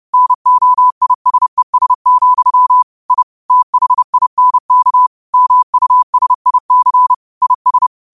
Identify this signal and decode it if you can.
Morse code, heard as a keyed tone, about 30 wpm, 1000 Hz. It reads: NOISES, I THINK MUSIC IS
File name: qotdfiles / 30wpm